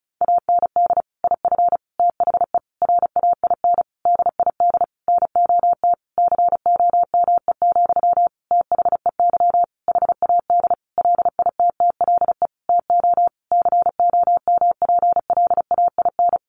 Morse code, heard as a keyed tone, about 34 wpm, 700 Hz. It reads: AND IF THE RAIN DID NOT COME, THEY HAD LITTLE TO COMPLAIN
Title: Sentences from Top 1300 Words 35wpm